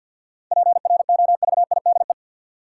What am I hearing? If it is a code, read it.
Morse code, heard as a keyed tone, about 50 wpm, 700 Hz. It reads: PROVIDE
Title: Sentences from Top 1000 Words - Review Part 2 - New Word Drill - 50wpm